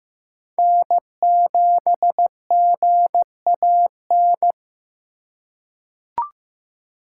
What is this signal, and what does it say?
Morse code, heard as a keyed tone, about 15 wpm, 700 Hz. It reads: N7GAN
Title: USA Call Signs - 5 Characters - 2x Character Spacing - 15wpm